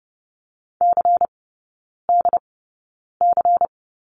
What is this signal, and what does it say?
Morse code, heard as a keyed tone, about 30 wpm, spaced as for 9 wpm, 690 Hz. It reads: CDC
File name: Lesson 16e - Letter c - Three Characters 30wpm